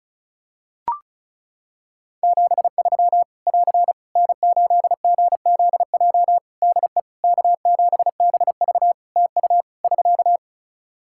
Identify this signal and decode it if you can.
Morse code, heard as a keyed tone, about 36 wpm, 700 Hz. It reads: 73 <AR> N8GZJ DE K7BV TU <SK>
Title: Practice QSO - Exchange 4 - Ending - 35wpm